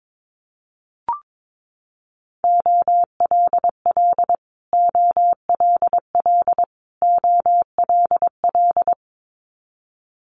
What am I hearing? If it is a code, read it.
Morse code, heard as a keyed tone, about 22 wpm, 700 Hz. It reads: OLL OLL OLL
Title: Multi-letter Phonemes Repeated 3x 22wpm